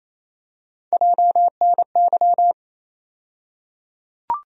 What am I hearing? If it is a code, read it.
Morse code, heard as a keyed tone, about 28 wpm, 700 Hz. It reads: JNY
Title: USA Call Sign Postfixes - 3 Letters 28wpm